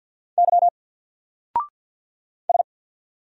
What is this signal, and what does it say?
Morse code, heard as a keyed tone, about 49 wpm, 700 Hz. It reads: Y S
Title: Single Letter-Number Alt 50wpm